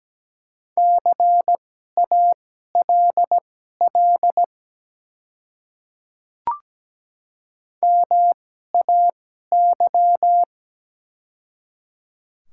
Morse code, heard as a keyed tone, about 17 wpm, spaced as for 12 wpm, 700 Hz. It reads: CALL MAY
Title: 2x Character Spacing - Top 100 Words - 17wpm